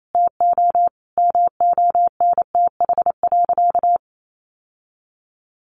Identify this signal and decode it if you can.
Morse code, heard as a keyed tone, about 28 wpm, 700 Hz. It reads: TO MONTH.
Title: Sentences from Top 1200 Words 28wpm